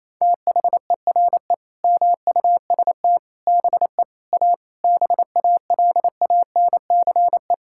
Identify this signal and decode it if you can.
Morse code, heard as a keyed tone, about 28 wpm, 700 Hz. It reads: THERE MUST BE A BALANCE
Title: Sentences from Top 1000 Words - Review Part 6 - 28wpm